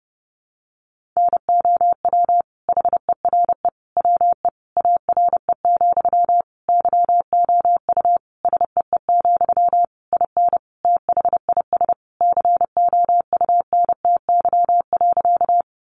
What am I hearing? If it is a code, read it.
Morse code, heard as a keyed tone, about 28 wpm, 700 Hz. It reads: NOW HERE WE ARE, YOU SEE, IN THIS COUNTY.
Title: Sentences from Top 1200 Words 30wpm